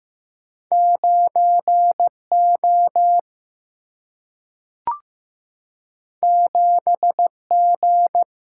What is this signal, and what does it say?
Morse code, heard as a keyed tone, about 15 wpm, 700 Hz. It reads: 9 O 7 G
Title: Single-Letter Single-Number 15wpm